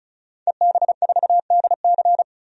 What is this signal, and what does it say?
Morse code, heard as a keyed tone, about 36 wpm, 700 Hz. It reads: EB4DC
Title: DXCC-100 Call Signs - 5 Characters - 35wpm